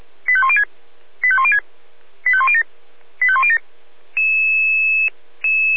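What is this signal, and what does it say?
Five-tone selective call (ZVEI), heard as quick runs of five tones, tones of about 70 ms. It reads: 85106 85106 85107 85107
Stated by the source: ZVEI five-tone selective call received on SDR.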